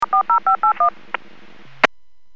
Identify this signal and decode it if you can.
DTMF, heard as telephone touch tones, tones of about 110 ms. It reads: 40581